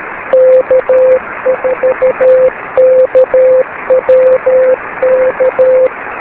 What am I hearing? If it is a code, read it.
Morse code, heard as a keyed tone, about 13 wpm, 520 Hz, with radio noise. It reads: K4KWK
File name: beacons / k4kwk-b